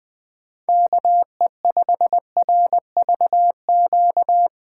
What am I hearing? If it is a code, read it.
Morse code, heard as a keyed tone, about 20 wpm, 700 Hz. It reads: KE5RVQ
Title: USA Call Signs - 6 Characters - 2x Character Spacing - 20wpm